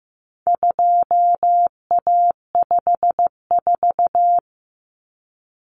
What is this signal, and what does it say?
Morse code, heard as a keyed tone, about 15 wpm, 700 Hz. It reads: 2A54
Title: Common Cut Numbers - 4-Digit 15wpm